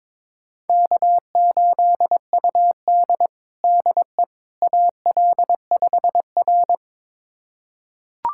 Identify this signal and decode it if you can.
Morse code, heard as a keyed tone, about 22 wpm, 700 Hz. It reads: K8UD DE AL5R
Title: USA Callsign DE Callsign - 4 Characters - 22wpm